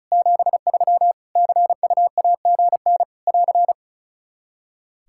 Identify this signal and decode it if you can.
Morse code, heard as a keyed tone, about 34 wpm, 700 Hz. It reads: 73 CUAGN <AR>
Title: Practice QSO - Exchange 4 - Ending - 35wpm